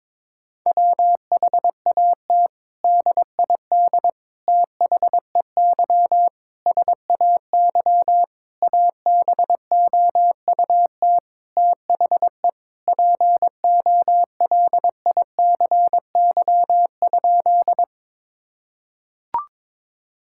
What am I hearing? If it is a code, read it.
Morse code, heard as a keyed tone, about 21 wpm, 700 Hz. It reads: WHAT DID THEY SAY ABOUT THE POLICY?
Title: Sentences from Top 1000 Words - Review Part 5 - 22wpm